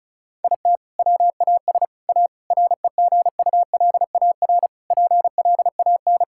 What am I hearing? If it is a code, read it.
Morse code, heard as a keyed tone, about 36 wpm, 700 Hz. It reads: IT WAS A REGULAR PLAN
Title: Sentences from Top 1000 Words - Review Part 6 - 35wpm